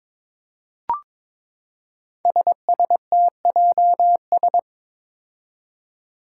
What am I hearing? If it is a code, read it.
Morse code, heard as a keyed tone, about 22 wpm, 700 Hz. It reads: SSTJS
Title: Five Challenging Letter-Number Combinations 22wpm